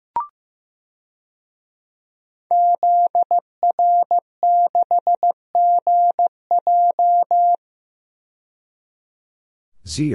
Morse code, heard as a keyed tone, about 15 wpm, 700 Hz. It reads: ZR6GJ
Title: DXCC-100 Call Signs - 5 Characters - 15wpm